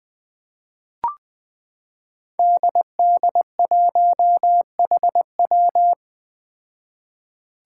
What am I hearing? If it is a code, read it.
Morse code, heard as a keyed tone, about 20 wpm, 700 Hz. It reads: DD1HW
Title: Five Challenging Letter-Number Combinations 20wpm